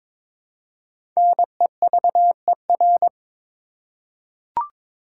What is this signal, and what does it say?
Morse code, heard as a keyed tone, about 22 wpm, 700 Hz. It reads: NEVER
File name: 5 Letter Words - First 3 Letters Given - 22wpm